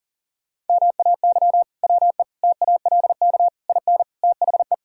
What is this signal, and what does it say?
Morse code, heard as a keyed tone, about 40 wpm, 700 Hz. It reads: MAY WE TALK IN THE
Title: Sentences from Top 1600 Words 40wpm